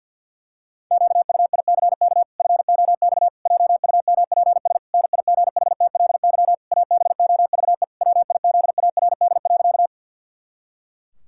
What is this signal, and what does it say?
Morse code, heard as a keyed tone, about 47 wpm, 700 Hz. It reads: QUICK FOX JUMPS NIGHTLY ABOVE WIZARD.
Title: Pangram Challenge v2 - 50wpm